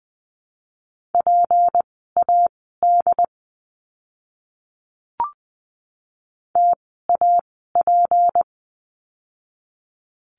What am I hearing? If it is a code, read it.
Morse code, heard as a keyed tone, about 20 wpm, spaced as for 15 wpm, 700 Hz. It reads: PAD TAP
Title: 153 Lesson 25f - Letter p - Words 20wpm